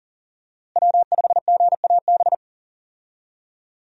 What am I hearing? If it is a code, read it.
Morse code, heard as a keyed tone, about 39 wpm, 700 Hz. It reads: W5GAB
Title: USA Call Signs - 5 Characters - Repeated 2x - 40wpm